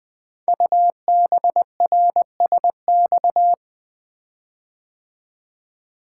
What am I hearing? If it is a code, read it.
Morse code, heard as a keyed tone, about 20 wpm, 700 Hz. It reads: UBRSX
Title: Code Groups - 5 Letters - 20wpm